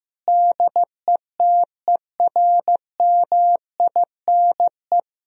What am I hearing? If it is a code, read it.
Morse code, heard as a keyed tone, about 15 wpm, 700 Hz. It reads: DETERMINE
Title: Sentences from Top 1000 Words - Review Part 4 - 15wpm